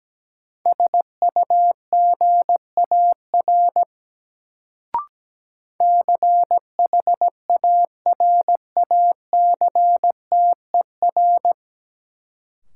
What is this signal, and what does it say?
Morse code, heard as a keyed tone, about 17 wpm, 700 Hz. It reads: SUGAR CHARACTER
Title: Sentences from Top 1200 Words - New Word Drill Alt - 17wpm